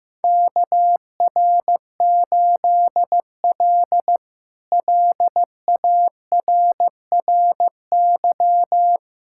Morse code, heard as a keyed tone, about 15 wpm, 700 Hz. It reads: KR8L LARRY
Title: K1USN SST QSO Practice - 15wpm